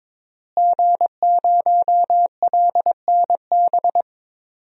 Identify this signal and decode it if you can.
Morse code, heard as a keyed tone, about 21 wpm, 700 Hz. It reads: G0LNB
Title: DXCC-100 Call Signs - 5 Characters - Repeated 3x - 22wpm